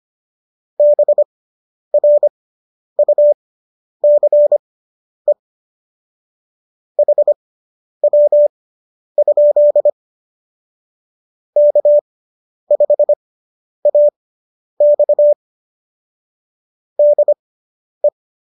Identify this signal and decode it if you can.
Morse code, heard as a keyed tone, about 25 wpm, spaced as for 10 wpm, 590 Hz. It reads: BRUCE HW? K5AX DE